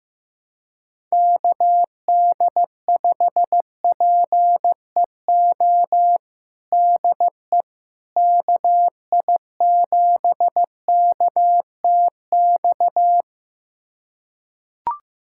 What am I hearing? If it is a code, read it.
Morse code, heard as a keyed tone, about 15 wpm, 700 Hz. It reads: KD5PEO DE KI7KTX
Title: USA Callsign DE Callsign - 6 Characters - 15wpm